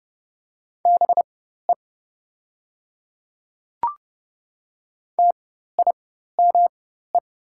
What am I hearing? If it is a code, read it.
Morse code, heard as a keyed tone, about 29 wpm, spaced as for 14 wpm, 700 Hz. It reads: BE TIME
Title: 4x Character Spacing - Top 100 Words - 30wpm